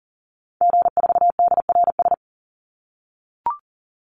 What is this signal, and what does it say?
Morse code, heard as a keyed tone, about 40 wpm, 700 Hz. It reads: G4DRS
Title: Call Signs - 40wpm